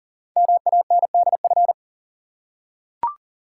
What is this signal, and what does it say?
Morse code, heard as a keyed tone, about 38 wpm, 700 Hz. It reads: MANDF
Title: Code Groups - 5 Letters - 40wpm